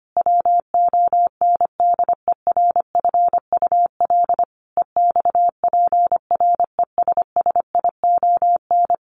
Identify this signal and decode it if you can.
Morse code, heard as a keyed tone, about 25 wpm, 700 Hz. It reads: WONDERFUL EXPRESSION
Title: Sentences from Top 1200 Words 25wpm